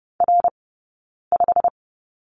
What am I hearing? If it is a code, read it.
Morse code, heard as a keyed tone, about 30 wpm, spaced as for 9 wpm, 690 Hz. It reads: R5
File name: course / 30:10wpm / 086 Lesson 15d - Number 5 - Two Characters 30wpm